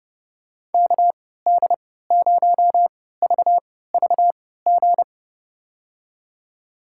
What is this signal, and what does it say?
Morse code, heard as a keyed tone, about 30 wpm, spaced as for 17 wpm, 700 Hz. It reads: KD0VVG
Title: USA Call Signs - 6 Characters - 3x Character Spacing - 30wpm